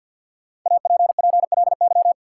Morse code, heard as a keyed tone, about 53 wpm, 700 Hz. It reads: APPLY
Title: Sentences from Top 1200 Words - New Word Drill Alt - 50wpm